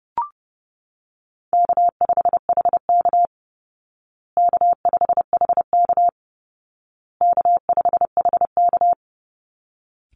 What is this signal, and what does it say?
Morse code, heard as a keyed tone, about 30 wpm, 700 Hz. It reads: K5HK K5HK K5HK
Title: Call Signs Repeated 2x - 30wpm